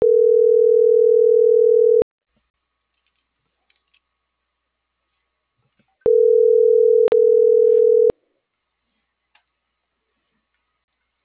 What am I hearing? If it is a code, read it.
Ringback tone (North America), 440 + 480 Hz.